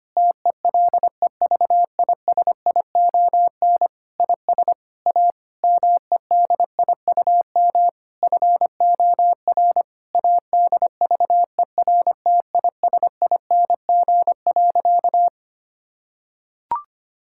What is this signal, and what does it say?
Morse code, heard as a keyed tone, about 25 wpm, 700 Hz. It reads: TELEVISION IS A MEDIUM FOR ADVERTISING.
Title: Sentences from Top 1000 Words - Review Part 6 - 25wpm